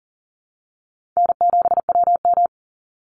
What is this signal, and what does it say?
Morse code, heard as a keyed tone, about 40 wpm, 700 Hz. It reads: N7WM
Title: NA SOTA Chaser Callsigns 40wpm